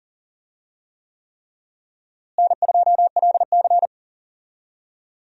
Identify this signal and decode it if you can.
Morse code, heard as a keyed tone, about 40 wpm, 700 Hz. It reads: N2LC
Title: USA Call Signs - 4 Characters - Repeated 3x - 40wpm